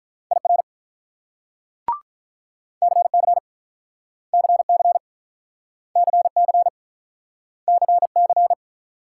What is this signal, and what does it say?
Morse code, heard as a keyed tone, about 45 wpm, 700 Hz. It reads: IR CC CC CC CC